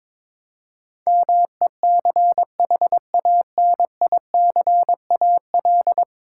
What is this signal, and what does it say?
Morse code, heard as a keyed tone, about 22 wpm, 700 Hz. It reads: MECHANICAL
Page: morse-code-ninja-practice-22wpm